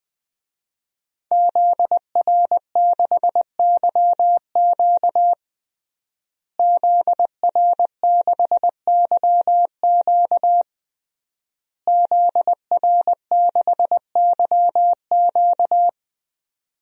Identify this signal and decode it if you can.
Morse code, heard as a keyed tone, about 20 wpm, 700 Hz. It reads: ZR6YQ ZR6YQ ZR6YQ
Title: DXCC-100 Call Signs - 5 Characters - Repeated 3x - 20wpm